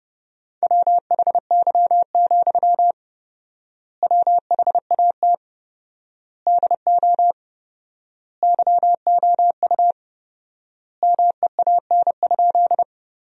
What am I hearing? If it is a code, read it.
Morse code, heard as a keyed tone, about 29 wpm, 700 Hz. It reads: WHY, WHAT DO YOU MEAN?
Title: 4x Word-Spacing - Sentences from Top 200 Words - 30wpm